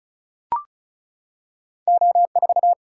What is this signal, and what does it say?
Morse code, heard as a keyed tone, about 36 wpm, 700 Hz. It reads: O4
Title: Single-Letter Single-Number 35wpm